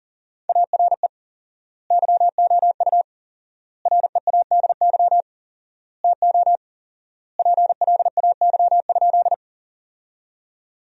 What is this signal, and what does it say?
Morse code, heard as a keyed tone, about 37 wpm, 700 Hz. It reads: ARE YOU READY TO PLAY?
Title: 4x Word-Spacing - Sentences from Top 400 Words - 40wpm